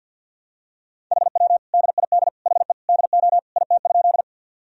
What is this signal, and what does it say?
Morse code, heard as a keyed tone, about 53 wpm, 700 Hz. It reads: HW DID HE DO IT?
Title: Sentences from Top 100 Words and Abbreviations 50wpm